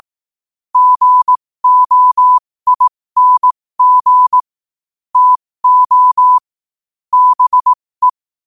Morse code, heard as a keyed tone, about 18 wpm, spaced as for 15 wpm, 1000 Hz. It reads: GOING TO BE
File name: qotdfiles / 15wpm